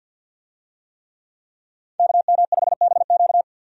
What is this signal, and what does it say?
Morse code, heard as a keyed tone, about 51 wpm, 700 Hz. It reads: KM5BQ
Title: USA Call Signs - 5 Characters - 50wpm